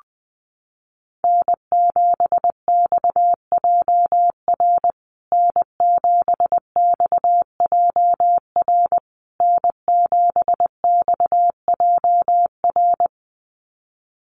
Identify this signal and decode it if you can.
Morse code, heard as a keyed tone, about 20 wpm, 700 Hz. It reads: N7XJR N7XJR N7XJR
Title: USA Call Signs Repeated 3x 20wpm